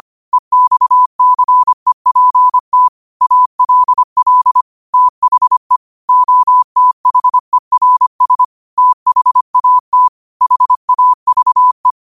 Morse code, heard as a keyed tone, about 25 wpm, 1000 Hz. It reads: EXCEPT ALL THE OTHERS THAT HAVE
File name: qotdfiles / 25wpm